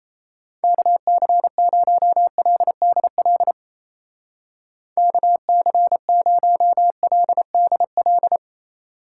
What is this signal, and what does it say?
Morse code, heard as a keyed tone, about 34 wpm, 700 Hz. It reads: KC0LDL KC0LDL
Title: Speed Racing - USA Call Signs 28wpm